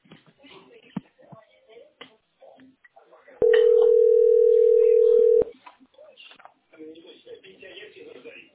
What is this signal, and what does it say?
Ringback tone (North America), 440 + 480 Hz, with speech.
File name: a524baed-178b-4f99-a11a